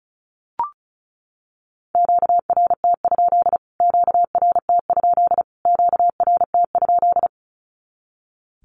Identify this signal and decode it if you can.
Morse code, heard as a keyed tone, about 34 wpm, 700 Hz. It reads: QRT? QRT? QRT?
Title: Q-Codes Repeated 3x 35wpm